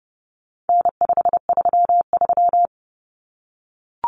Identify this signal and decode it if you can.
Morse code, heard as a keyed tone, about 30 wpm, 700 Hz. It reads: N533
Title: Common Cut Numbers - 4-Digit 30wpm